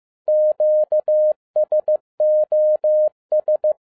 Morse code, heard as a keyed tone, about 15 wpm, 610 Hz. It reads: QSOS
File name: MY CW LIFE 1 15wpm